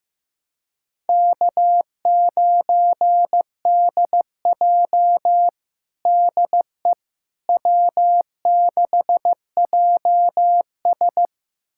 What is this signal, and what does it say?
Morse code, heard as a keyed tone, about 15 wpm, 700 Hz. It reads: K9DJ DE W6JS
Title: USA Callsign DE Callsign - 4 Characters - 15wpm